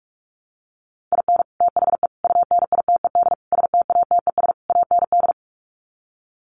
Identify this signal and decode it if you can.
Morse code, heard as a keyed tone, about 44 wpm, 700 Hz. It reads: IN THE UNITED STATES AND
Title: Sets of 5 Words 45wpm